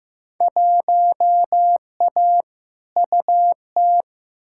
Morse code, heard as a keyed tone, about 15 wpm, 700 Hz. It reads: 1A UT
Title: Field Day Exchanges - 15wpm